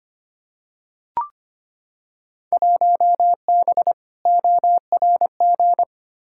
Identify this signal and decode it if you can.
Morse code, heard as a keyed tone, about 25 wpm, 700 Hz. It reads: 1B ORG
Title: Field Day Exchanges Encore - 25wpm